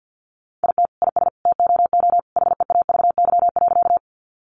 Morse code, heard as a keyed tone, about 50 wpm, 700 Hz. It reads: IT IS TOO HEAVY.
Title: Sentences from Top 1100 Words 50wpm